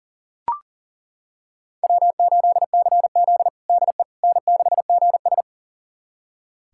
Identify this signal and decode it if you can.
Morse code, heard as a keyed tone, about 37 wpm, 700 Hz. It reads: W8CZ DE N6GS
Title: USA Callsign DE Callsign - 4 Characters - 40wpm